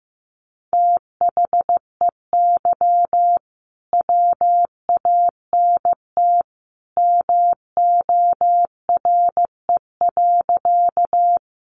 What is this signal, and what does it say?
Morse code, heard as a keyed tone, about 15 wpm, 700 Hz. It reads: THEY WANT MORE.
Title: Sentences from Top 200 Words 15wpm